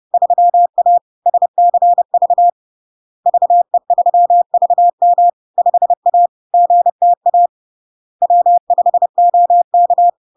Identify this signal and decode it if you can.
Morse code, heard as a keyed tone, about 30 wpm, 700 Hz. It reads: 3A SCV VE3VM 5A GTA W5OK
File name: LARK-FD-2022-Log-in-CW-30wpm-700-Hz